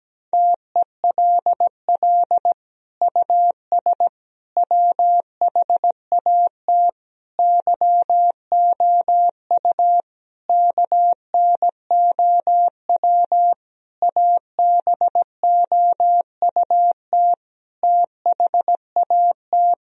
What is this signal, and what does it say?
Morse code, heard as a keyed tone, about 17 wpm, 700 Hz. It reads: TELL US WHAT YOU KNOW ABOUT THAT
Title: Sentences from Top 1000 Words - Review Part 4 - 17wpm